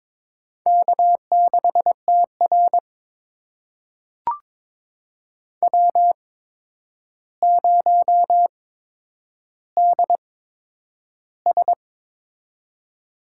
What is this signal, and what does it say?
Morse code, heard as a keyed tone, about 22 wpm, spaced as for 9 wpm, 700 Hz. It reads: K6TR W0DS
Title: USA Call Signs - 4 Characters - 8x Character Spacing - 22wpm